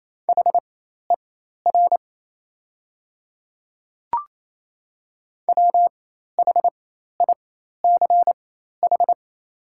Morse code, heard as a keyed tone, about 28 wpm, spaced as for 13 wpm, 690 Hz. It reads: HER WHICH
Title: 4x Character Spacing - Top 100 Words - 28wpm